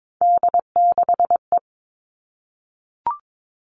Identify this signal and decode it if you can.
Morse code, heard as a keyed tone, about 22 wpm, 700 Hz. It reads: D6E
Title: All Cut Numbers - 3-Digit 22wpm